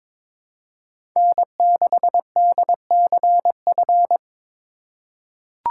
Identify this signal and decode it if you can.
Morse code, heard as a keyed tone, about 21 wpm, 700 Hz. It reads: N6DCF
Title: USA Call Signs - 5 Characters - 3x Character Spacing - 22wpm